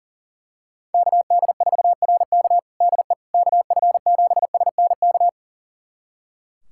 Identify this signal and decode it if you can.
Morse code, heard as a keyed tone, about 37 wpm, 700 Hz. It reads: KD4RK DE KF7SNK
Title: USA Callsign DE Callsign 40wpm